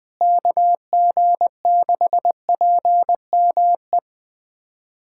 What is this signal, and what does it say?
Morse code, heard as a keyed tone, about 20 wpm, 700 Hz. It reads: KG6PME
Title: USA Callsign DE Callsign 20wpm